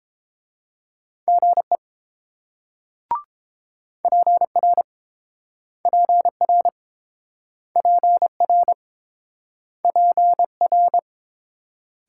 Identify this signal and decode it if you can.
Morse code, heard as a keyed tone, about 28 wpm, 700 Hz. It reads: GE PR PR PR PR